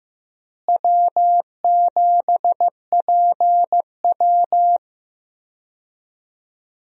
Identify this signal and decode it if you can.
Morse code, heard as a keyed tone, about 14 wpm, 700 Hz. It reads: W7PW
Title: USA Call Signs - 4 Characters - Repeated 3x - 15wpm